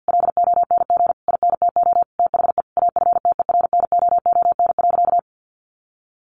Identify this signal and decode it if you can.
Morse code, heard as a keyed tone, about 50 wpm, 700 Hz. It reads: LONG INTO THE AFTERNOON.
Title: Sentences from Top 1500 Words 50wpm